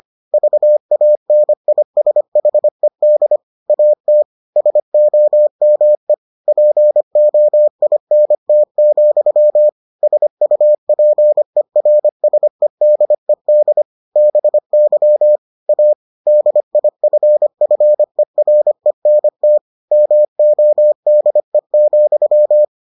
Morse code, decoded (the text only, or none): VANISHED AT SOME POINT, SUPERSEDED BY A DIFFERENT MODE,